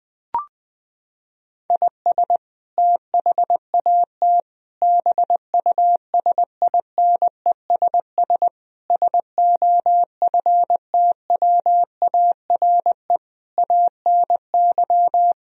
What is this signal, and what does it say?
Morse code, heard as a keyed tone, about 21 wpm, 700 Hz. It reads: IS THAT BUSINESS SOFTWARE ANY
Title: Sentences from Top 1000 Words - Review Part 3 - 20wpm